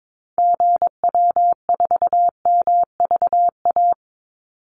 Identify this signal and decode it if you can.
Morse code, heard as a keyed tone, about 22 wpm, 700 Hz. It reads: GW4MVA
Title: Call Signs - 22wpm